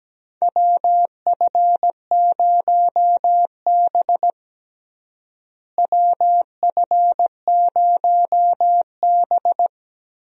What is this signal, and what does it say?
Morse code, heard as a keyed tone, about 17 wpm, 700 Hz. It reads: WF0B WF0B
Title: USA Call Signs - 4 Characters - Repeated 3x - 17wpm